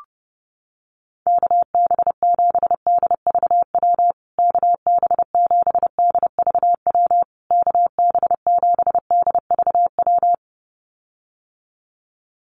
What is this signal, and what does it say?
Morse code, heard as a keyed tone, about 30 wpm, 700 Hz. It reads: KB7DVW KB7DVW KB7DVW
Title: USA Call Signs Repeated 3x 30wpm